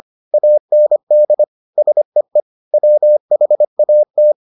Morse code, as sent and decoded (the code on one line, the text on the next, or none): .- -. -.. / ... . / .-- .... .- -
AND SEE WHAT